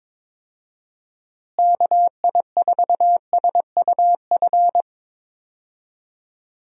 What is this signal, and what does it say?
Morse code, heard as a keyed tone, about 22 wpm, 700 Hz. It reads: KI4SUF
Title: USA Call Signs - 6 Characters - Repeated 2x - 22wpm